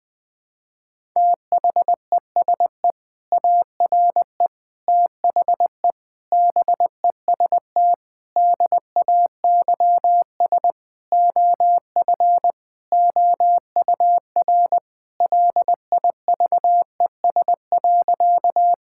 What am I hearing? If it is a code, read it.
Morse code, heard as a keyed tone, about 20 wpm, 700 Hz. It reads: THESE ARE THE BEST DAYS OF OUR LIVES.
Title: Sentences with 2-Word Phrases - Spoken Beginning - 20wpm